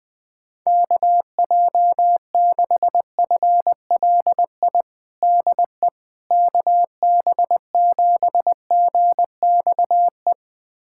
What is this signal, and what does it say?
Morse code, heard as a keyed tone, about 21 wpm, 700 Hz. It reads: KJ6FLI DE KB7GXE
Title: USA Callsign DE Callsign - 6 Characters - 20wpm